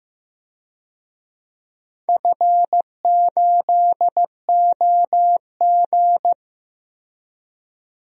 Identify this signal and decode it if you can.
Morse code, heard as a keyed tone, about 15 wpm, 700 Hz. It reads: F8OG
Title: DXCC-100 Call Signs - 4 Characters - 15wpm